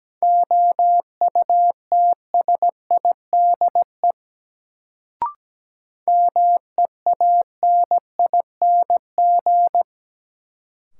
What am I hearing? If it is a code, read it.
Morse code, heard as a keyed tone, about 17 wpm, 700 Hz. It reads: OUTSIDE MEANING
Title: Sentences from Top 900 Words - New Word Drill - 17wpm